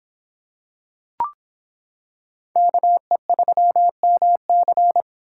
Morse code, heard as a keyed tone, about 26 wpm, 700 Hz. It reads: KE3MC